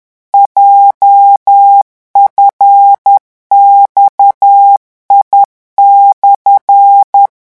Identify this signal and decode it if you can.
Morse code, heard as a keyed tone, about 10 wpm, 800 Hz. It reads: JFXI/
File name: morse mp3 / revision17-vitesse 10 mots